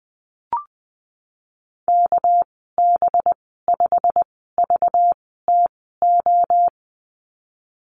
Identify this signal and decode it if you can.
Morse code, heard as a keyed tone, about 20 wpm, spaced as for 15 wpm, 700 Hz. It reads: KB5VTO
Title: 219 Lesson 35g - Number 0 - Call Signs 20wpm